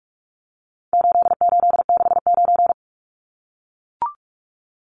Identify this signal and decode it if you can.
Morse code, heard as a keyed tone, about 44 wpm, 700 Hz. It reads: 8869
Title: Numbers - 4-Digit 45wpm